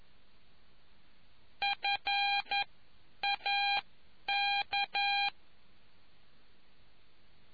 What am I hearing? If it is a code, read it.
Morse code, heard as a keyed tone, about 11 wpm, spaced as for 9 wpm, 800 Hz. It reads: FAK